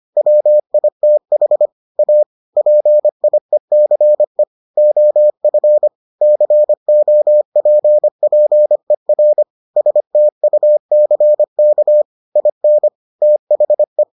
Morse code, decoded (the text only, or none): WITH A PIECE OF COPPER STUCK IN THE